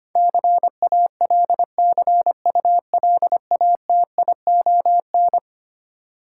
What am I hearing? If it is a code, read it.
Morse code, heard as a keyed tone, about 25 wpm, 700 Hz. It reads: CALCULATION
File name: practice / 25wpm / Eleven Letter Words - 25wpm